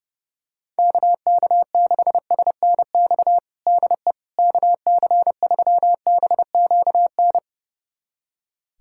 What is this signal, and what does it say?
Morse code, heard as a keyed tone, about 30 wpm, 700 Hz. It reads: KK6SNX DE KC3BQN
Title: USA Callsign DE Callsign - 6 Characters - 30wpm